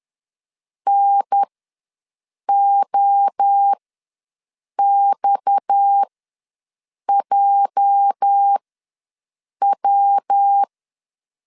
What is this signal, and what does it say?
Morse code, heard as a keyed tone, about 10 wpm, spaced as for 6 wpm, 800 Hz. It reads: NOXJW
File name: LEZIONI / 9 2 ws 10wpm 4130